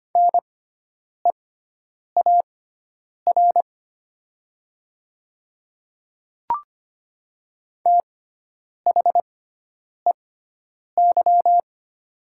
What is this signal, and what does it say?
Morse code, heard as a keyed tone, about 25 wpm, spaced as for 9 wpm, 690 Hz. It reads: NEAR THEY
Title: 6x Character Spacing - Top 200 Words - 25wpm